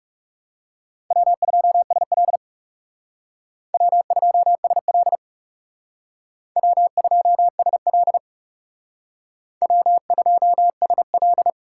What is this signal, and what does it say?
Morse code, heard as a keyed tone, about 38 wpm, 700 Hz. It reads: W2SL W2SL W2SL W2SL